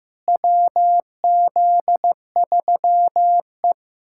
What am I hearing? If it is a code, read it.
Morse code, heard as a keyed tone, about 15 wpm, 700 Hz. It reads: WZ3E
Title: USA Callsign DE Callsign - 4 Characters - 15wpm